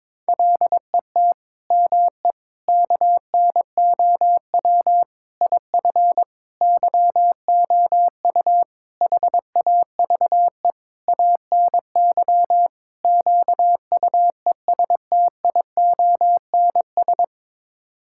Morse code, heard as a keyed tone, about 22 wpm, 700 Hz. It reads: LET ME KNOW IF YOU HAVE ANY QUESTIONS
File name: practice / 22wpm / Sets of 8 Words - 22wpm